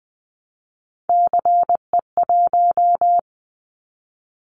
Morse code, heard as a keyed tone, about 20 wpm, 700 Hz. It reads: CE1
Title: Callsign Prefixes for Top 100 DXCC - 20wpm